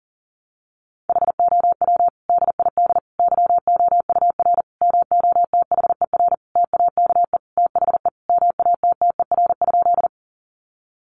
Morse code, heard as a keyed tone, about 37 wpm, 700 Hz. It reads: HOW DID YOUR MOTHER TAKE THE MATTER?
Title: Sentences from Top 700 Words 40wpm